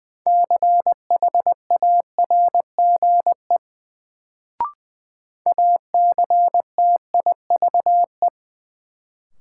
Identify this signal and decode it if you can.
Morse code, heard as a keyed tone, about 20 wpm, 700 Hz. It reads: CHARGE ACTIVE
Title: Sentences from Top 900 Words - New Word Drill - 20wpm